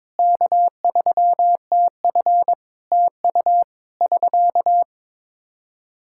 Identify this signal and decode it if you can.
Morse code, heard as a keyed tone, about 22 wpm, 700 Hz. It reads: K3TF TU <SK>
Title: Practice QSO - Exchange 4 - Ending - 22wpm